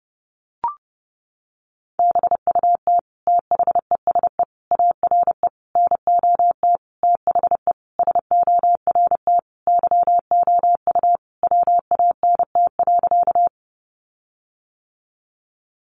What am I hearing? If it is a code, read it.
Morse code, heard as a keyed tone, about 30 wpm, 700 Hz. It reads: BUT THESE ARE NOT THE SORT YOU WANT.
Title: Sentences from Top 1300 Words 30wpm